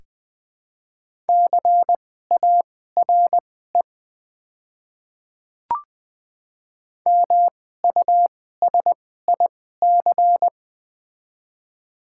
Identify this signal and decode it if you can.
Morse code, heard as a keyed tone, about 20 wpm, spaced as for 14 wpm, 700 Hz. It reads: CARE MUSIC
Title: 2x Character Spacing - Top 300 Words - 20wpm